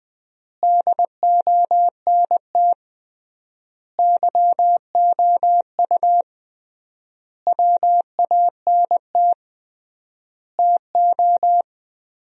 Morse code, heard as a keyed tone, about 20 wpm, 700 Hz. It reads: DONT YOU WANT TO
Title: 3x Word-Spacing - Sentences from Top 300 Words - 20wpm